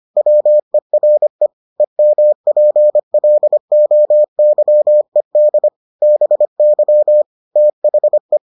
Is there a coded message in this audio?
Yes: Morse code from 0.2 to 8.4 s, heard as a keyed tone, about 25 wpm, 600 Hz. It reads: WERE EMPLOYED BY THE